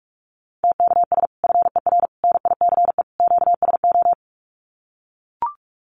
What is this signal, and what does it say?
Morse code, heard as a keyed tone, about 44 wpm, 700 Hz. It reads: TKS FER NICE QSO
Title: Common QSO Phrases 45wpm